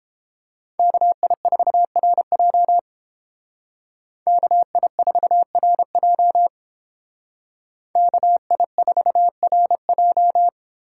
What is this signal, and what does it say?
Morse code, heard as a keyed tone, about 30 wpm, 700 Hz. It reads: KI4RJ KI4RJ KI4RJ